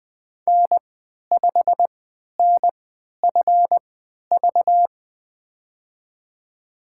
Morse code, heard as a keyed tone, about 20 wpm, spaced as for 11 wpm, 700 Hz. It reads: N5NFV
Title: USA Call Signs - 5 Characters - 3x Character Spacing - 20wpm